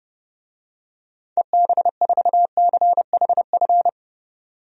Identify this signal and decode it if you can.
Morse code, heard as a keyed tone, about 30 wpm, 700 Hz. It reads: EB4CHF
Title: DXCC-100 Call Signs - 6 Characters - 30wpm